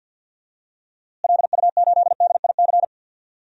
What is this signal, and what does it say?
Morse code, heard as a keyed tone, about 52 wpm, 700 Hz. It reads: LU8DIC
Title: DXCC-100 Call Signs - 6 Characters - Repeated 2x - 50wpm